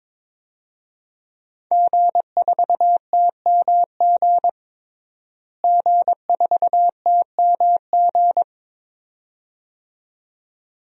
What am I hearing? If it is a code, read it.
Morse code, heard as a keyed tone, about 22 wpm, 700 Hz. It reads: G4TMG G4TMG
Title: DXCC-100 Call Signs - 5 Characters - Repeated 2x - 22wpm